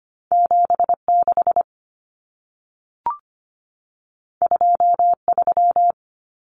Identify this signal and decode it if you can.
Morse code, heard as a keyed tone, about 25 wpm, 700 Hz. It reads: 76 23
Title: Common Cut Numbers - 2-Digit 25wpm